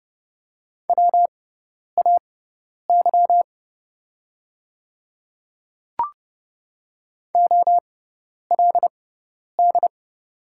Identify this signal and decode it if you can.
Morse code, heard as a keyed tone, about 30 wpm, spaced as for 10 wpm, 700 Hz. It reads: WAY OLD
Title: 6x Character Spacing - Top 200 Words - 30wpm